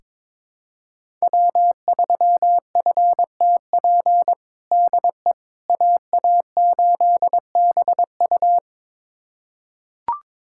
Morse code, heard as a keyed tone, about 22 wpm, 700 Hz. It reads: W3FTP DE AA8BU
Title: USA Callsign DE Callsign - 5 Characters - 22wpm